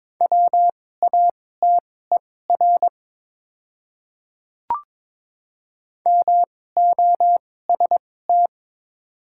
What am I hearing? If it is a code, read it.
Morse code, heard as a keyed tone, about 22 wpm, spaced as for 16 wpm, 700 Hz. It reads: WATER MOST
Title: 2x Character Spacing - Top 100 Words - 22wpm